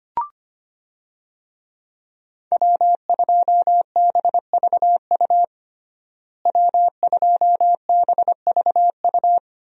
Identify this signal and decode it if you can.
Morse code, heard as a keyed tone, about 25 wpm, 700 Hz. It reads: W2BVU W2BVU
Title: USA Call Signs - 5 Characters - Repeated 2x - 25wpm